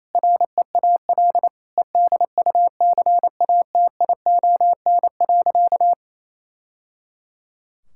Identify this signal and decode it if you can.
Morse code, heard as a keyed tone, about 28 wpm, 700 Hz. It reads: REAL EDUCATION.
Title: Sentences from Top 1000 Words - Review Part 6 - 28wpm